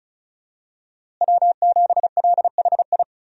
Morse code, heard as a keyed tone, about 34 wpm, 700 Hz. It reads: W7LHI
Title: USA Call Signs - 5 Characters - Repeated 3x - 35wpm